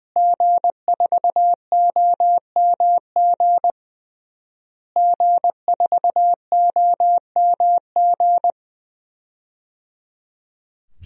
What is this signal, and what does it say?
Morse code, heard as a keyed tone, about 20 wpm, 700 Hz. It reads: G4OMG G4OMG
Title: DXCC-100 Call Signs - 5 Characters - Repeated 3x - 20wpm